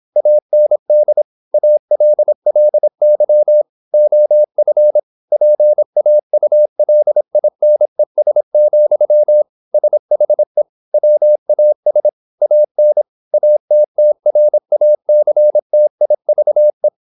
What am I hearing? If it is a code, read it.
Morse code, heard as a keyed tone, about 27 wpm, 600 Hz. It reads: AND ALLY OF PAULINES, SHE WAS AN ATTRACTIVE
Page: Never 26wpm | CW med Gnister